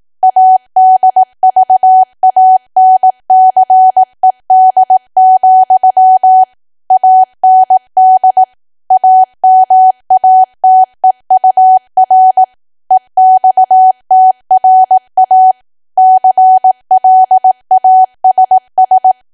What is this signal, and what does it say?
Morse code, heard as a keyed tone, about 18 wpm, 750 Hz. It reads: ADVANCED, AND AMATEUR EXTRA CLASS
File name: cwa / W1AW-Level2 / 18WPM